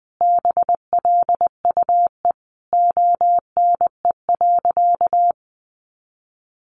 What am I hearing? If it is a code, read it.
Morse code, heard as a keyed tone, about 20 wpm, 700 Hz. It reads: BLUE ONE.
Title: Additional Easy Phrases 20wpm